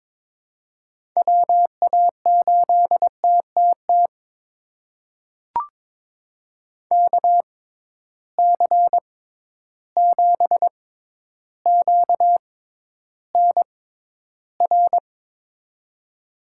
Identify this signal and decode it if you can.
Morse code, heard as a keyed tone, about 23 wpm, spaced as for 11 wpm, 700 Hz. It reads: WA8TTT KC7QNR
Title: USA Call Signs - 6 Characters - 6x Character Spacing - 22wpm